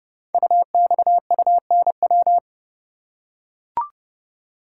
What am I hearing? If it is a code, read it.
Morse code, heard as a keyed tone, about 28 wpm, 700 Hz. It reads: UXUNW
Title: Code Groups - 5 Letters - 30wpm